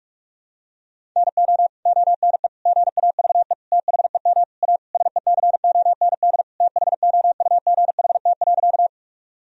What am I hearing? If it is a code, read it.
Morse code, heard as a keyed tone, about 46 wpm, 700 Hz. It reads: NO ONE GAVE THEM A SECOND THOUGHT.
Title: Sentences from Top 1000 Words - Review Part 1 - 45wpm